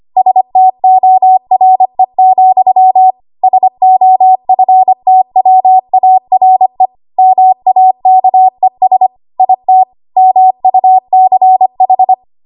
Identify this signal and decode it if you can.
Morse code, heard as a keyed tone, about 25 wpm, 750 Hz. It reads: STORE, SOFTWARE MAKES IT MUCH
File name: Archive / 25 WPM / 25WPM